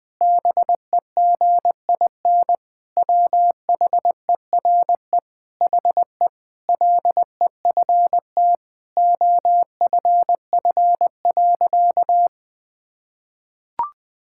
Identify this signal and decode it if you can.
Morse code, heard as a keyed tone, about 20 wpm, 700 Hz. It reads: BEGIN WHERE HE LEFT OFF.
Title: Sentences from Top 1000 Words - Review Part 6 - 20wpm